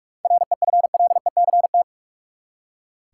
Morse code, heard as a keyed tone, about 45 wpm, 700 Hz. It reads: REFLECT
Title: Seven Letter Words - 45wpm